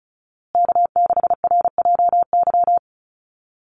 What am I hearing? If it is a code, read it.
Morse code, heard as a keyed tone, about 34 wpm, 700 Hz. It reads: K6RJY
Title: USA Call Signs Repeated 2x 35wpm